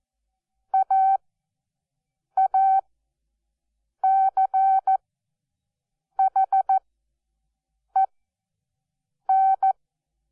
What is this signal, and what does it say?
Morse code, heard as a keyed tone, about 14 wpm, spaced as for 6 wpm, 770 Hz. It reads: AACHEN
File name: Onlinelehrgang / morsen / DJ4UF Morsen12